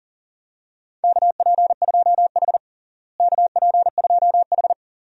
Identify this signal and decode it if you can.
Morse code, heard as a keyed tone, about 40 wpm, 700 Hz. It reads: KP2H KP2H
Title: DXCC-100 Call Signs - 4 Characters - Repeated 2x - 40wpm